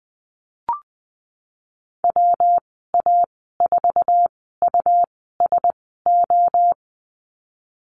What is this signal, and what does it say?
Morse code, heard as a keyed tone, about 20 wpm, spaced as for 15 wpm, 700 Hz. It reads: WA4USO
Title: 154 Lesson 25g - Letter p - Call Signs 20wpm